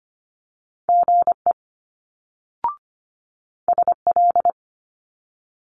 Morse code, heard as a keyed tone, about 25 wpm, 700 Hz. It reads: GE SL
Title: Multi-letter Phonemes Quick Fire 25wpm